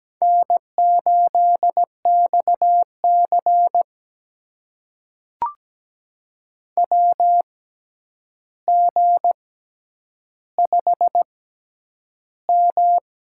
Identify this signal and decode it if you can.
Morse code, heard as a keyed tone, about 17 wpm, spaced as for 9 wpm, 700 Hz. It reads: N8XC WG5M
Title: USA Call Signs - 6x Character Spacing - 17wpm